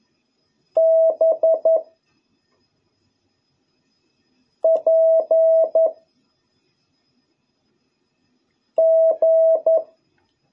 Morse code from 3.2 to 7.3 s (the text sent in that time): P